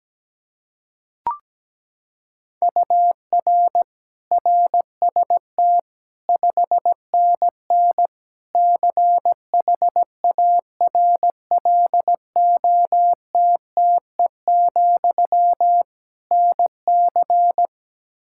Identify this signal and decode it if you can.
Morse code, heard as a keyed tone, about 17 wpm, 700 Hz. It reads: UR RST 5NN CHARLOTTE, NC
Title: K1USN SST QSO Practice 17wpm